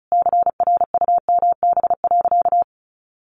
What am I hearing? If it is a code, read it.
Morse code, heard as a keyed tone, about 34 wpm, 700 Hz. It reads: CRUMB.
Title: Additional Easy Phrases 35wpm